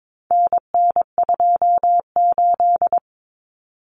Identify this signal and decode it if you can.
Morse code, heard as a keyed tone, about 22 wpm, 700 Hz. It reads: NN28
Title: Common Cut Numbers - 4-Digit 22wpm